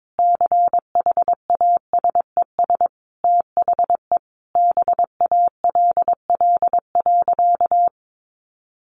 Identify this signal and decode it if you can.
Morse code, heard as a keyed tone, about 22 wpm, 700 Hz. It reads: CHASES THE BALL.
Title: Additional Easy Phrases 22wpm